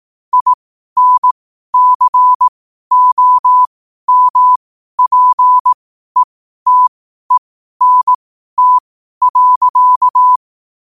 Morse code, heard as a keyed tone, about 18 wpm, spaced as for 12 wpm, 1000 Hz. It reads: INCOMPETENT.
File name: qotdfiles / 12wpm